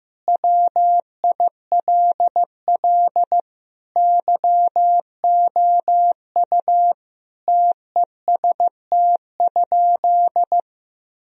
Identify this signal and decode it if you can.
Morse code, heard as a keyed tone, about 16 wpm, 700 Hz. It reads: WILL YOU TEST?
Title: Sentences from Top 1000 Words - Review Part 3 - 15wpm